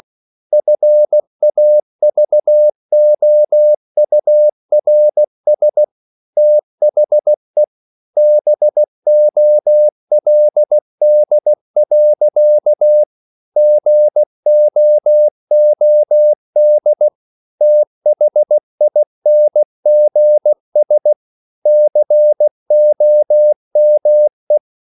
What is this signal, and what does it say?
Morse code, heard as a keyed tone, about 16 wpm, 600 Hz. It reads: FAVOURS THE BOLD. GOOD THINGS COME
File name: CW / Practice / EnglishSayings16